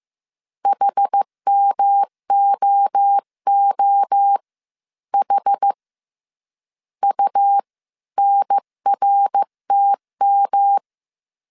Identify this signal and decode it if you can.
Morse code, heard as a keyed tone, about 15 wpm, spaced as for 12 wpm, 800 Hz. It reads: HMOOH UNRTM
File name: CW / DET / 1 3 1500C S C 14wpm 2458